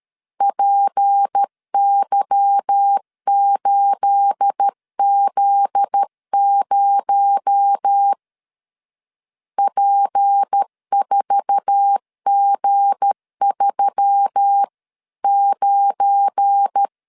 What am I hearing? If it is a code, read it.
Morse code, heard as a keyed tone, about 13 wpm, 800 Hz. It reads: PY8Z0 P4G39
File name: CW / DET / 1 9 1500C S C 12wpm 3811